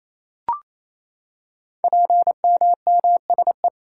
Morse code, heard as a keyed tone, about 27 wpm, 700 Hz. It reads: PMMSE
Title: Code Groups - 5 Letters - 28wpm